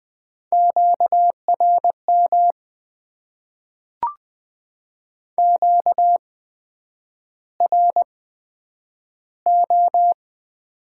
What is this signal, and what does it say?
Morse code, heard as a keyed tone, about 20 wpm, spaced as for 9 wpm, 700 Hz. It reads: QRM QRO
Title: Top 90 QSO Elements - 8x Character Spacing - 20wpm